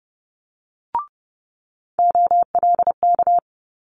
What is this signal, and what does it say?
Morse code, heard as a keyed tone, about 30 wpm, 700 Hz. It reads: OLK
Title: Multi-letter Phonemes Quick Fire 30wpm